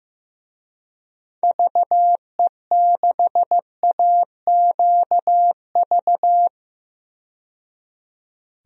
Morse code, heard as a keyed tone, about 15 wpm, 700 Hz. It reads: VE6AQV
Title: DXCC-100 Call Signs - 6 Characters - 15wpm